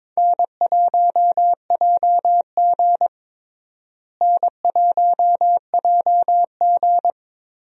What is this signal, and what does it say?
Morse code, heard as a keyed tone, about 22 wpm, 700 Hz. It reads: N1JG N1JG
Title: USA Call Signs - 4 Characters - Repeated 3x - 22wpm